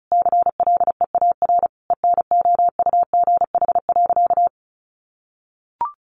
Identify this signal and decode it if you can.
Morse code, heard as a keyed tone, about 34 wpm, 700 Hz. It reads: CLEAR ENOUGH.
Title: Sentences from Top 1500 Words 35wpm